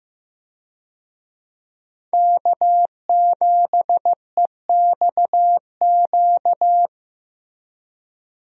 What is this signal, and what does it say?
Morse code, heard as a keyed tone, about 15 wpm, 700 Hz. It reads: K7EXQ
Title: USA Call Signs - 5 Characters - Repeated 3x - 15wpm